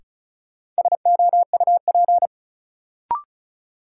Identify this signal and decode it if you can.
Morse code, heard as a keyed tone, about 34 wpm, 700 Hz. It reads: SOUP
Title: Sentences from Top 1000 Words - Review Part 6 - New Word Drill - 35wpm